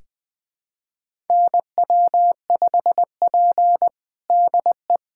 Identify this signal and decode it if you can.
Morse code, heard as a keyed tone, about 20 wpm, 700 Hz. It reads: NW5P DE
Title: USA Callsign DE Callsign - 4 Characters - 20wpm